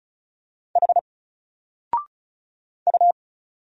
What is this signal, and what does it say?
Morse code, heard as a keyed tone, about 36 wpm, 700 Hz. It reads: H U
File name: Single-Letter-Number-35wpm